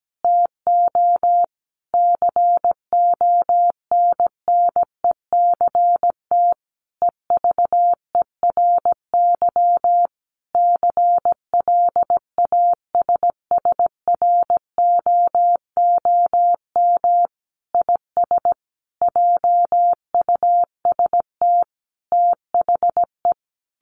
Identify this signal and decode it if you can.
Morse code, heard as a keyed tone, about 17 wpm, 700 Hz. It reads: TO CONNECT EVERY CLASSROOM IS JUST THE
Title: Sentences from Top 1500 Words 17wpm